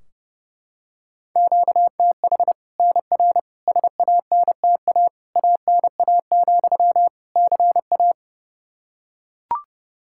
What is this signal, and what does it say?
Morse code, heard as a keyed tone, about 30 wpm, 700 Hz. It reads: QTH NR SANTA ANA, CA
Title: Practice QSO - Exchange 1 - Beginning - 30wpm